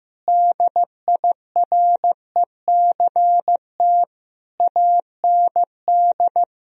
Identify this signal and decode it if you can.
Morse code, heard as a keyed tone, about 15 wpm, 700 Hz. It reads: DIRECT AND
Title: Sentences from Top 1000 Words - Review Part 4 - 15wpm